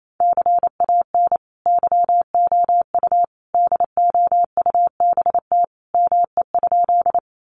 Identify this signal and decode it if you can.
Morse code, heard as a keyed tone, about 28 wpm, 700 Hz. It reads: CAN YOU DOUBT ME?
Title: Sentences from Top 1500 Words 28wpm